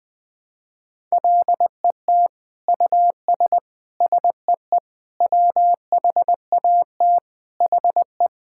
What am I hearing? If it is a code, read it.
Morse code, heard as a keyed tone, about 20 wpm, 700 Hz. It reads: LET US SEE WHAT HE
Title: Sentences from Top 1000 Words - Review Part 5 - 20wpm